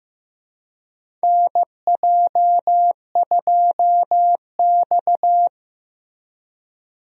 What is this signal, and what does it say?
Morse code, heard as a keyed tone, about 15 wpm, 700 Hz. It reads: NJ2X
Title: USA Call Signs - 4 Characters - Repeated 3x - 15wpm